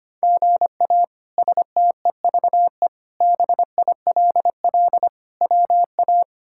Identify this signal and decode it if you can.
Morse code, heard as a keyed tone, about 25 wpm, 700 Hz. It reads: GA STEVE BILL WA
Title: K1USN SST QSO Practice - 25wpm